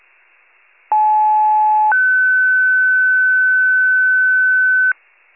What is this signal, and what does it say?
Two-tone page, heard as a one-second tone, then a three-second tone, about 860 then 1500 Hz. It is heard from 0.9 to 4.9 s.
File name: pager / 75 FIRE / wcalert